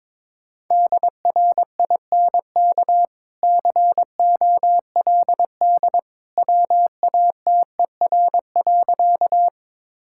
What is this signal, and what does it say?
Morse code, heard as a keyed tone, about 22 wpm, 690 Hz. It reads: DRINK COLD WATER.
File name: practice / 22wpm / Easy Sentences with Hints - 22wpm